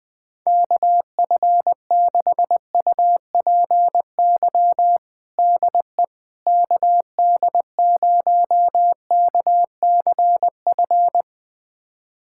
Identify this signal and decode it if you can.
Morse code, heard as a keyed tone, about 20 wpm, 700 Hz. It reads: KF6UPY DE KD0KCF
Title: USA Callsign DE Callsign - 6 Characters - 20wpm